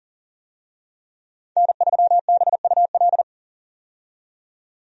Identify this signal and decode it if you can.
Morse code, heard as a keyed tone, about 40 wpm, 700 Hz. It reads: N3BUL
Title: USA Call Signs - 5 Characters - 40wpm